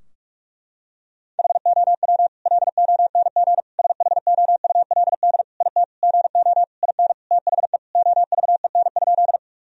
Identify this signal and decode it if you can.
Morse code, heard as a keyed tone, about 48 wpm, 700 Hz. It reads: HOW LONG SHOULD IT GO IN THE OVEN?
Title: Sentences from Top 1000 Words - Review Part 4 - 45wpm